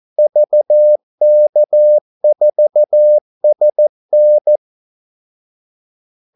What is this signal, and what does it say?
Morse code, heard as a keyed tone, about 14 wpm, 600 Hz. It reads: VK4SN
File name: Callsigns-14wpm